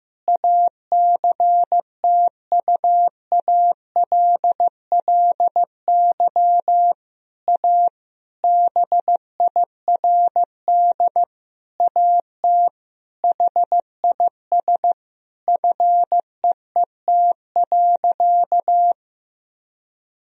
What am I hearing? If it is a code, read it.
Morse code, heard as a keyed tone, about 15 wpm, 700 Hz. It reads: ACTUALLY A BIRD AT HIS FEET.
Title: Sentences from Top 1000 Words - Review Part 4 - 15wpm